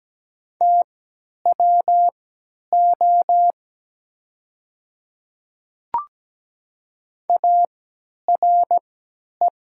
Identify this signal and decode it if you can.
Morse code, heard as a keyed tone, about 17 wpm, spaced as for 10 wpm, 700 Hz. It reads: TWO ARE
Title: 3x Character Spacing - Top 100 Words - 17wpm